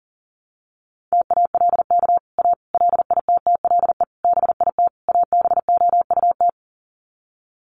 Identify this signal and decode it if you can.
Morse code, heard as a keyed tone, about 40 wpm, 700 Hz. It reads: TALK A LITTLE BIT ABOUT
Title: Sets of 5 Words Encore 40wpm